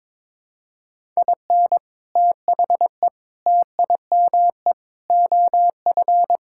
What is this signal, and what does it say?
Morse code, heard as a keyed tone, about 21 wpm, 700 Hz. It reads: IN THE TIME OF
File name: practice / 22wpm / Sentences with 3-Word Phrases - 22wpm